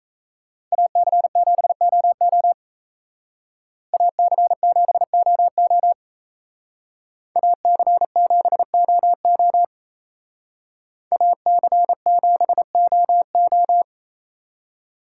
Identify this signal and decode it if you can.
Morse code, heard as a keyed tone, about 36 wpm, 700 Hz. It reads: AC7OO AC7OO AC7OO AC7OO